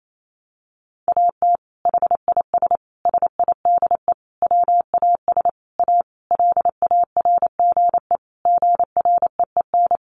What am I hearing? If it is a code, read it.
Morse code, heard as a keyed tone, about 28 wpm, 700 Hz. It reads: AT HIS SIDE WAS A LARGE GREEN
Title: Sentences from Top 500 Words 28wpm